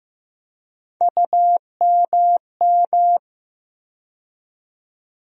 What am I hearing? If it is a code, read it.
Morse code, heard as a keyed tone, about 15 wpm, 700 Hz. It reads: UMM
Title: USA Call Sign Postfixes - 3 Letters 15wpm